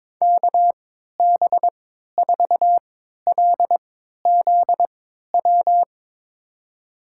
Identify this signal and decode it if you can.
Morse code, heard as a keyed tone, about 22 wpm, spaced as for 12 wpm, 700 Hz. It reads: KB4LZW
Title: USA Call Signs - 3x Character Spacing - 22wpm